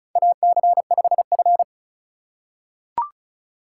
Morse code, heard as a keyed tone, about 34 wpm, 700 Hz. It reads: AC5F
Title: USA Call Signs - 4 Characters - 3x Character Spacing - 35wpm